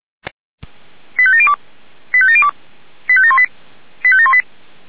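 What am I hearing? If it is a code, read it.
Five-tone selective call (ZVEI), heard as quick runs of five tones, tones of about 70 ms. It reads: 85702 85702 85718 85718
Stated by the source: ZVEI five-tone selective call received by SDR.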